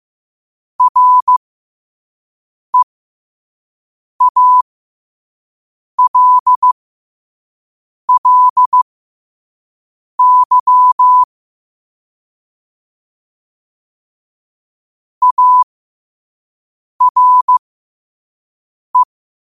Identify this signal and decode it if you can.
Morse code, heard as a keyed tone, about 15 wpm, spaced as for 5 wpm, 1000 Hz. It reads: REALLY ARE
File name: qotdfiles / 5wpm